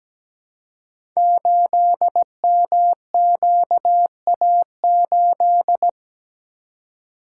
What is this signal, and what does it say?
Morse code, heard as a keyed tone, about 17 wpm, 700 Hz. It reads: 8MQA8
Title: Code Groups - 5 Letter:Numbers - 17wpm